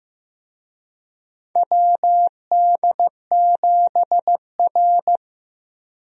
Morse code, heard as a keyed tone, about 15 wpm, 700 Hz. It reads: WD7R
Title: USA Call Signs - 4 Characters - Repeated 3x - 15wpm